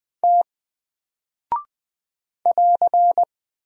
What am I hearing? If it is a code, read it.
Morse code, heard as a keyed tone, about 21 wpm, 700 Hz. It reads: T <AR>
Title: Single Character (Letter:Number:Punctuation:Prosign) Alt 20wpm